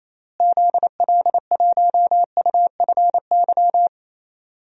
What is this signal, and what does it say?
Morse code, heard as a keyed tone, about 28 wpm, 700 Hz. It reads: ZL1UFY
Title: DXCC-100 Call Signs - 6 Characters - Repeated 2x - 28wpm